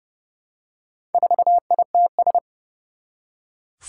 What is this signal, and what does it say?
Morse code, heard as a keyed tone, about 30 wpm, 700 Hz. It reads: 4ITS
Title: Four Challenging Letter-Number Combinations 30wpm